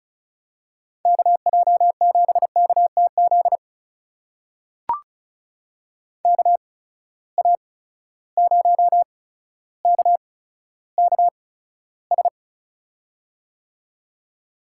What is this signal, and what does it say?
Morse code, heard as a keyed tone, about 36 wpm, spaced as for 15 wpm, 700 Hz. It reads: KJ7KTZ KA0KKS
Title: USA Call Signs - 8x Character Spacing - 35wpm